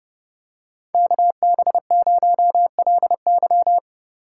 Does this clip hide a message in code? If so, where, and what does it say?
Yes: Morse code from 0.9 to 3.8 s, heard as a keyed tone, about 28 wpm, 700 Hz. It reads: KB0LY